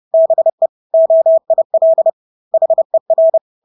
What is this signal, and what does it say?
Morse code, heard as a keyed tone, about 30 wpm, 650 Hz. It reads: BE OIL HER
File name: Engelske ord 30-30 wpm